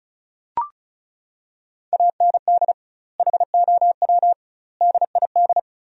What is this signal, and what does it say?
Morse code, heard as a keyed tone, about 34 wpm, 700 Hz. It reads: AND HOW DID
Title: 2x Word-Spacing - Sentences from Top 100 Words - 35wpm